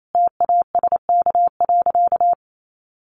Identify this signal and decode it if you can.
Morse code, heard as a keyed tone, about 28 wpm, 700 Hz. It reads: TASK.
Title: Sentences from Top 1400 Words 28wpm